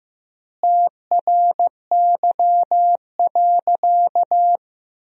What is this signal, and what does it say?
Morse code, heard as a keyed tone, about 15 wpm, 700 Hz. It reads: TRY.
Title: 3x Word-Spacing - Sentences from Top 200 Words - 15wpm